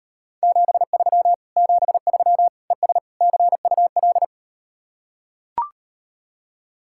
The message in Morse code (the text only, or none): 73 73 ES CUL